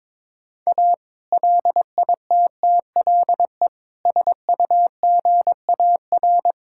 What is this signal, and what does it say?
Morse code, heard as a keyed tone, about 22 wpm, 700 Hz. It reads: A LITTLE SUGAR
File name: practice / 22wpm / Sentences with 2-Word Phrases - 22wpm